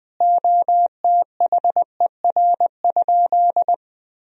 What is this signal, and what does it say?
Morse code, heard as a keyed tone, about 21 wpm, 700 Hz. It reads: OTHER?
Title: Sentences from Top 1000 Words - Review Part 1 - 20wpm